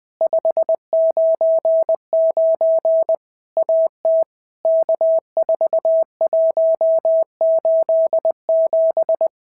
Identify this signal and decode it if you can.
Morse code, heard as a keyed tone, about 20 wpm, 650 Hz. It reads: 599 AT K4187
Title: POTA QSO Practice - 20wpm